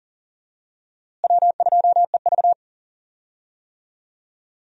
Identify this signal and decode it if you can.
Morse code, heard as a keyed tone, about 40 wpm, 700 Hz. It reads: W2EV
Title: USA Call Signs - 4 Characters - Repeated 2x - 40wpm